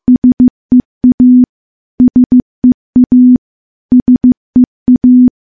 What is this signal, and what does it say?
Morse code, heard as a keyed tone, about 15 wpm, 250 Hz. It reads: SEA SEA SEA
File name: morse-training-15WPM